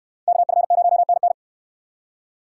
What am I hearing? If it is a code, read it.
Morse code, heard as a keyed tone, about 63 wpm, 700 Hz. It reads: KF0NN